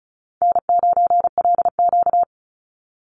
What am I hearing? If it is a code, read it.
Morse code, heard as a keyed tone, about 34 wpm, 700 Hz. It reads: N9LQ
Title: Easy Call Signs Repeated 2x - 35wpm